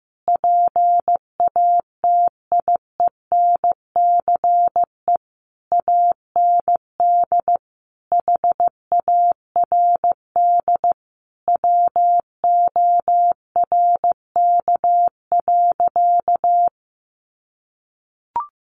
Morse code, heard as a keyed tone, about 14 wpm, 700 Hz. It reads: PATIENCE AND HARD WORK.
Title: Sentences from Top 1300 Words 15wpm